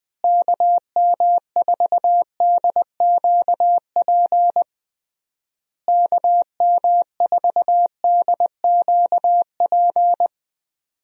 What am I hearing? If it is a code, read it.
Morse code, heard as a keyed tone, about 20 wpm, 700 Hz. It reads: KM4DQP KM4DQP
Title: USA Call Signs - 6 Characters - Repeated 3x - 20wpm